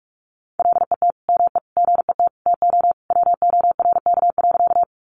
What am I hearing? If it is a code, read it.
Morse code, heard as a keyed tone, about 44 wpm, 700 Hz. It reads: LET ME GET TO WORK.
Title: Sentences from Top 300 Words 45wpm